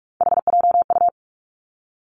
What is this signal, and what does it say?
Morse code, heard as a keyed tone, about 46 wpm, 700 Hz. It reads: HJU
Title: Three Challenging Letter-Number Combinations 45wpm